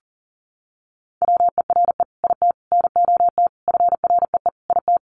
Morse code, heard as a keyed tone, about 40 wpm, 700 Hz. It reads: WERE IT NOT FREE IT
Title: Sentences from Top 900 Words 40wpm